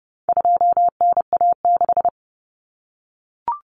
Morse code, heard as a keyed tone, about 30 wpm, 700 Hz. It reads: 2NA6
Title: Common Cut Numbers - 4-Digit 30wpm